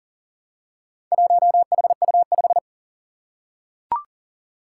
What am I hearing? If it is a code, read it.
Morse code, heard as a keyed tone, about 40 wpm, 700 Hz. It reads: 1HU5
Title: Four Challenging Letter-Number Combinations 40wpm